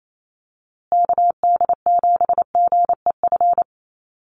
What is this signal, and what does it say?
Morse code, heard as a keyed tone, about 28 wpm, 700 Hz. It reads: KD7GEF
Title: USA Call Signs Repeated 3x 28wpm